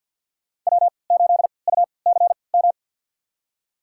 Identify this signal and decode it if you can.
Morse code, heard as a keyed tone, about 49 wpm, spaced as for 28 wpm, 700 Hz. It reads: W8UCM
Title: USA Call Signs - 5 Characters - 3x Character Spacing - 50wpm